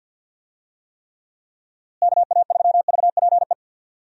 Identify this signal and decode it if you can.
Morse code, heard as a keyed tone, about 52 wpm, 700 Hz. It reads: KA3VPE
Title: USA Call Signs - 6 Characters - Repeated 3x - 50wpm